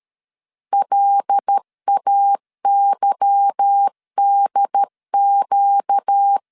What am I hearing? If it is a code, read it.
Morse code, heard as a keyed tone, about 13 wpm, 800 Hz. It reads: LAYDQ
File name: LEZIONI / 8 YJ45 / 8 Y 12wpm 2228